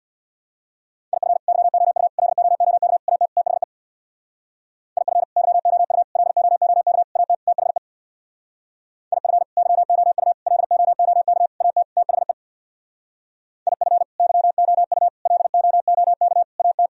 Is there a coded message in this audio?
Yes: Morse code from 1.1 to 16.9 s, heard as a keyed tone, about 64 wpm, 700 Hz. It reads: IF YOU LOOK AT THE IF YOU LOOK AT THE IF YOU LOOK AT THE IF YOU LOOK AT